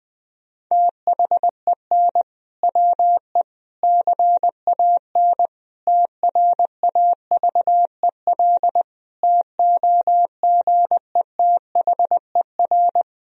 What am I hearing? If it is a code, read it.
Morse code, heard as a keyed tone, about 21 wpm, 700 Hz. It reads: THEN WE CAN TRAVEL TOGETHER
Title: Sentences from Top 1000 Words - Review Part 2 - 20wpm